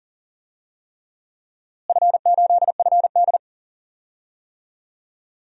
Morse code, heard as a keyed tone, about 40 wpm, 700 Hz. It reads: F8FD
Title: DXCC-100 Call Signs - 4 Characters - Repeated 3x - 40wpm